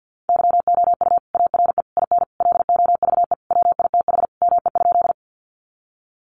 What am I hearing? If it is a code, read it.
Morse code, heard as a keyed tone, about 50 wpm, 700 Hz. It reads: YOU ARE IN LOVE WITH ME?
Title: Sentences from Top 400 Words 50wpm